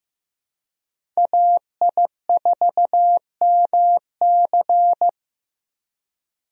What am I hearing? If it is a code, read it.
Morse code, heard as a keyed tone, about 15 wpm, 700 Hz. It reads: AI4MC
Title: USA Call Signs - 5 Characters - Repeated 3x - 15wpm